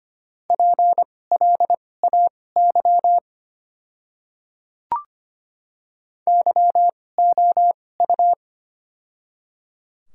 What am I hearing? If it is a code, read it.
Morse code, heard as a keyed tone, about 25 wpm, spaced as for 18 wpm, 700 Hz. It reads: PLAY YOU
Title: 2x Character Spacing - Top 200 Words - 25wpm